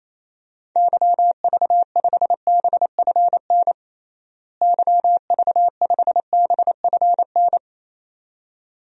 Morse code, heard as a keyed tone, about 28 wpm, 700 Hz. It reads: YV5BFN YV5BFN
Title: DXCC-100 Call Signs - 6 Characters - Repeated 2x - 28wpm